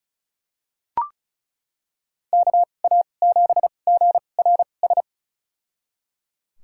Morse code, heard as a keyed tone, about 36 wpm, spaced as for 25 wpm, 700 Hz. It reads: KA7GRS
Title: USA Call Signs - 2x Character Spacing - 35wpm